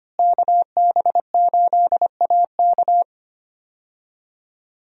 Morse code, heard as a keyed tone, about 25 wpm, 700 Hz. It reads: KB8AK
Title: USA Call Signs - 5 Characters - Repeated 3x - 25wpm